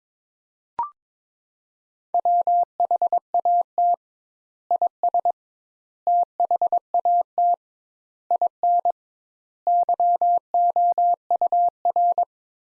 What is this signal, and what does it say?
Morse code, heard as a keyed tone, about 22 wpm, 700 Hz. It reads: WHAT IS THAT IN YOUR
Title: Sentences Top 200 Words - 22wpm